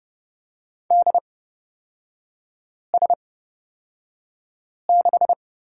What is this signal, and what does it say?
Morse code, heard as a keyed tone, about 30 wpm, spaced as for 5 wpm, 700 Hz. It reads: DS6
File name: Lesson 36e - BT - Three Characters 30wpm